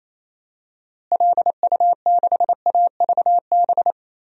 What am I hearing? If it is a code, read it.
Morse code, heard as a keyed tone, about 28 wpm, 700 Hz. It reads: LU6AVB
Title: DXCC-100 Call Signs - 6 Characters - Repeated 3x - 28wpm